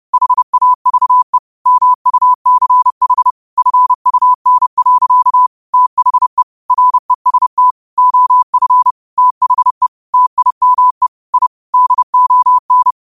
Morse code, heard as a keyed tone, about 30 wpm, 1000 Hz. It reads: HAVE MUCH FUN. THE REST OF THE TIME I DON